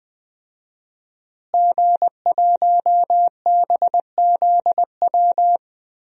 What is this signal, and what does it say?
Morse code, heard as a keyed tone, about 20 wpm, 700 Hz. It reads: G1BZW
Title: DXCC-100 Call Signs - 5 Characters - Repeated 3x - 20wpm